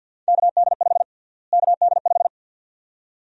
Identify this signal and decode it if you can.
Morse code, heard as a keyed tone, about 53 wpm, 700 Hz. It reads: KD5 KD5
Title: USA Call Sign Prefixes - Repeated 2x - 50wpm